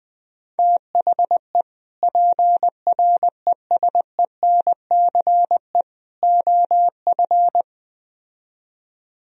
Morse code, heard as a keyed tone, about 20 wpm, 700 Hz. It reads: THE PRESENCE OF
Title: Speed-Racing - Sets of 3 Words 17wpm